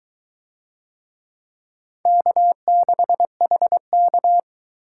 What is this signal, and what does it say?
Morse code, heard as a keyed tone, about 20 wpm, 700 Hz. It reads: K6HK